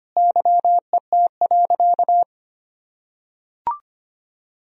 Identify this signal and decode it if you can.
Morse code, heard as a keyed tone, about 25 wpm, 700 Hz. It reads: YET.
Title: Sentences from Top 1000 Words - Review Part 3 - 25wpm